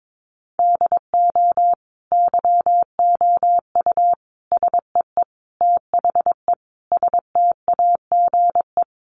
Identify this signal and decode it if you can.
Morse code, heard as a keyed tone, about 22 wpm, 700 Hz. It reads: DO YOU SEE THE STAGE
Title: Sentences from Top 1200 Words 22wpm